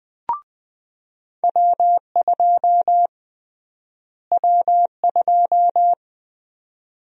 Morse code, heard as a keyed tone, about 21 wpm, 700 Hz. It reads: W2 W2
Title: USA Call Sign Prefixes - Repeated 2x - 20wpm